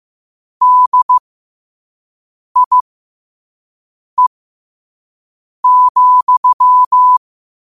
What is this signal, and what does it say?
Morse code, heard as a keyed tone, about 15 wpm, spaced as for 5 wpm, 1000 Hz. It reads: DIE,
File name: qotdfiles / 5wpm